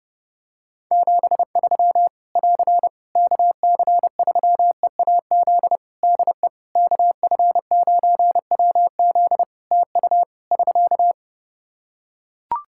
Morse code, heard as a keyed tone, about 30 wpm, 700 Hz. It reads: 73 <AR> KC3EAZ DE KF9WZ TU <SK>
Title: Practice QSO - Exchange 4 - Ending - 30wpm